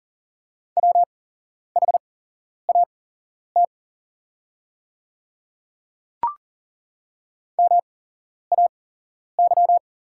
Morse code, heard as a keyed tone, about 41 wpm, spaced as for 11 wpm, 700 Hz. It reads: WHAT MAY
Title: 8x Character Spacing - Top 100 Words - 40wpm